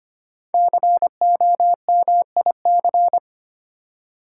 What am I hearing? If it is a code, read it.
Morse code, heard as a keyed tone, about 25 wpm, 700 Hz. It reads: COMIC
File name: 5 Letter Words - First 3 Letters Given - 25wpm